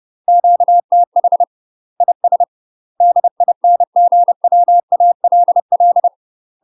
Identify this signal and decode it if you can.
Morse code, heard as a keyed tone, about 30 wpm, 700 Hz. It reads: QTH IS DINGWALL
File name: UK MASTER QSOs TRAINING/30WPM